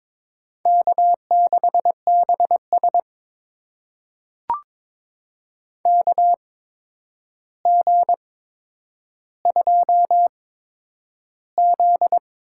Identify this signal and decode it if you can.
Morse code, heard as a keyed tone, about 21 wpm, spaced as for 9 wpm, 700 Hz. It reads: K6BS KG2Z
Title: USA Call Signs - 4 Characters - 8x Character Spacing - 22wpm